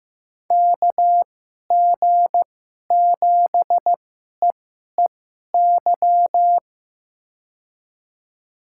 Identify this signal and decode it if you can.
Morse code, heard as a keyed tone, about 15 wpm, spaced as for 11 wpm, 700 Hz. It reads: KG7EEY
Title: USA Call Signs - 2x Character Spacing - 15wpm